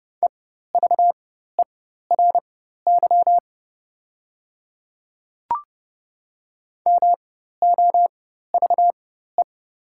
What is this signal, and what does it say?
Morse code, heard as a keyed tone, about 30 wpm, spaced as for 14 wpm, 700 Hz. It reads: EVERY MOVE
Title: 4x Character Spacing - Top 300 Words - 30wpm